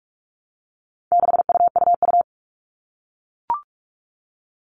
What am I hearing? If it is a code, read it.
Morse code, heard as a keyed tone, about 44 wpm, 700 Hz. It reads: 6UUU
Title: All Cut Numbers - 4-Digit 45wpm